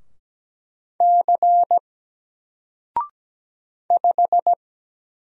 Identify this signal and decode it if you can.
Morse code, heard as a keyed tone, about 17 wpm, 700 Hz. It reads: C 5
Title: Single Letter-Number 17wpm